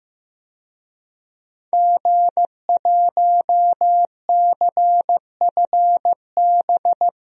Morse code, heard as a keyed tone, about 15 wpm, 700 Hz. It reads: G1CFB
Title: DXCC-100 Call Signs - 5 Characters - Repeated 2x - 15wpm